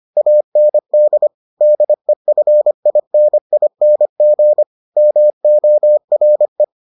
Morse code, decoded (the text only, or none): AND DEFINING MORE